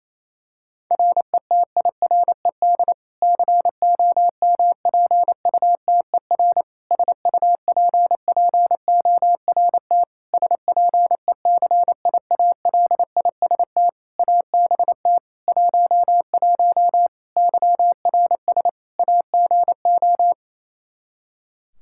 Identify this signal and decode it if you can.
Morse code, heard as a keyed tone, about 28 wpm, 700 Hz. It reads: RETIRED COMPUTER SUPPORT SPECIALIST ABT 11 YRS AGO
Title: Practice QSO - Exchange 3 - More Info - 28wpm